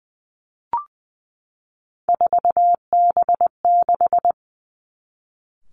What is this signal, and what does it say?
Morse code, heard as a keyed tone, about 20 wpm, 700 Hz. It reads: 4B6
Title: Three Challenging Letter-Number Combinations 20wpm